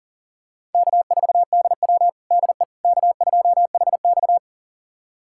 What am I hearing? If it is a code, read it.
Morse code, heard as a keyed tone, about 40 wpm, 700 Hz. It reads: K4DW DE K2HX
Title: USA Callsign DE Callsign - 4 Characters - 40wpm